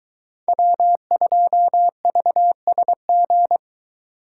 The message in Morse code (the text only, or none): W2VSG